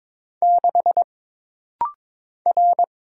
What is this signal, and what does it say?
Morse code, heard as a keyed tone, about 22 wpm, 700 Hz. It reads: 6 R
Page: This rapid-fire timing is geared to push your ICR (Instant Character Recognition) skills.